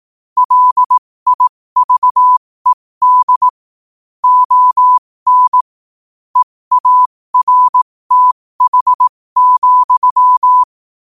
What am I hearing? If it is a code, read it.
Morse code, heard as a keyed tone, about 18 wpm, spaced as for 15 wpm, 1000 Hz. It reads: LIVED ON EARTH,
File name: qotdfiles / 15wpm